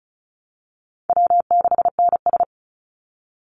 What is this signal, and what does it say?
Morse code, heard as a keyed tone, about 34 wpm, 700 Hz. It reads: W6NS
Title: Call Signs Repeated 3x - 35wpm